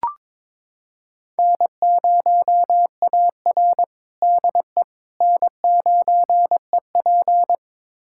Morse code, heard as a keyed tone, about 22 wpm, 700 Hz. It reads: N0AR DE N9EP
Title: USA Callsign DE Callsign - 4 Characters - 22wpm